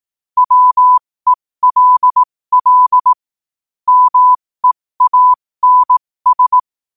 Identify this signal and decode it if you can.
Morse code, heard as a keyed tone, about 18 wpm, spaced as for 15 wpm, 1000 Hz. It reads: WELL MEANS
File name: qotdfiles / 15wpm